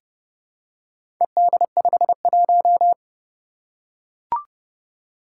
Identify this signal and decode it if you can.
Morse code, heard as a keyed tone, about 30 wpm, 700 Hz. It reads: ED51
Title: Four Challenging Letter-Number Combinations 30wpm